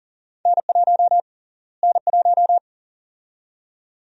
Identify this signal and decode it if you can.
Morse code, heard as a keyed tone, about 40 wpm, 700 Hz. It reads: N1 N1
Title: USA Call Sign Prefixes - Repeated 2x - 40wpm